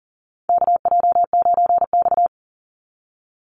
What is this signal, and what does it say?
Morse code, heard as a keyed tone, about 40 wpm, 700 Hz. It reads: KJ9X
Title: NA SOTA Chaser Callsigns 40wpm